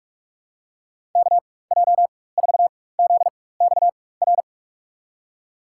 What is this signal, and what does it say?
Morse code, heard as a keyed tone, about 46 wpm, spaced as for 21 wpm, 700 Hz. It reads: KJ4ZXR
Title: USA Call Signs - 6 Characters - 4x Character Spacing - 45wpm